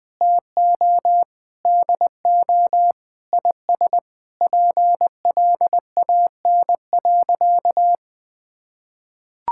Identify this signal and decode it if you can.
Morse code, heard as a keyed tone, about 20 wpm, 700 Hz. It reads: TO DO IS PLAN.
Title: Sentences with 8-Word Phrases - 20wpm